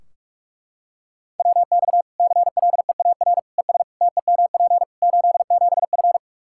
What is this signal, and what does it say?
Morse code, heard as a keyed tone, about 44 wpm, 700 Hz. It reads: WX CLEAR ES TEMP 87F
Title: Practice QSO - Exchange 2 - Info - 45wpm